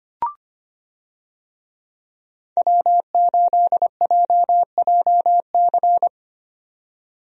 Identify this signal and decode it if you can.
Morse code, heard as a keyed tone, about 25 wpm, 700 Hz. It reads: W8JJC
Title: USA Call Signs - 5 Characters - 25wpm